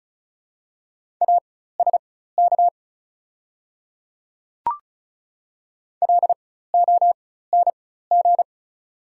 Morse code, heard as a keyed tone, about 34 wpm, spaced as for 16 wpm, 690 Hz. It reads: ASK LONG
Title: 4x Character Spacing - Top 300 Words - 35wpm